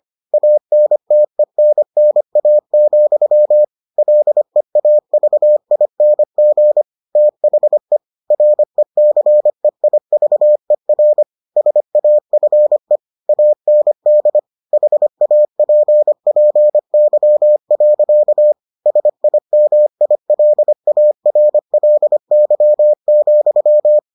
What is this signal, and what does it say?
Morse code, heard as a keyed tone, about 25 wpm, 600 Hz. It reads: ANTENNA, LEAVING THE RECEIVER SAFE AND HAPPY. SIMILARLY,